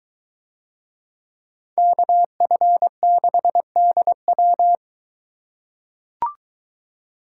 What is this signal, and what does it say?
Morse code, heard as a keyed tone, about 23 wpm, 700 Hz. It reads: KF6DW